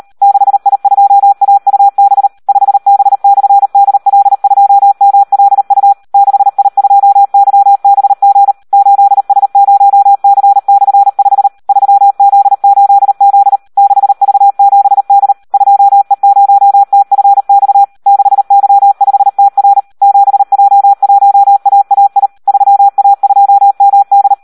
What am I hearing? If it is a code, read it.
Morse code, heard as a keyed tone, about 38 wpm, 800 Hz. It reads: 6I2AUB 5B/DL2MLU 6I2YBG 8S0C/5 3Z8Z 6V7D 2E0TFX 6Y5TR 7J1AAI 3A2MD
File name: Audio practica CW / SD 1 38